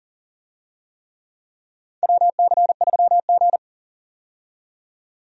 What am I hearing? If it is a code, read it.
Morse code, heard as a keyed tone, about 38 wpm, 700 Hz. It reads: WC3G
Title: USA Call Signs - 4 Characters - 40wpm